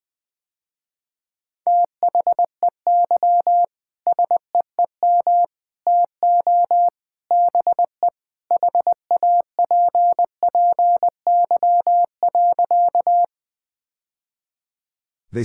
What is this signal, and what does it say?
Morse code, heard as a keyed tone, about 21 wpm, 700 Hz. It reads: THEY SEEM TO BE HAPPY.
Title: Sentences with 3-Word Phrases - Spoken Beginning - 20wpm